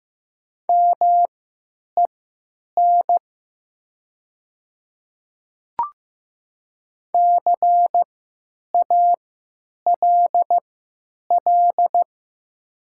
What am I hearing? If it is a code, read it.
Morse code, heard as a keyed tone, about 15 wpm, spaced as for 9 wpm, 700 Hz. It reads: MEN CALL
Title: 3x Character Spacing - Top 300 Words - 15wpm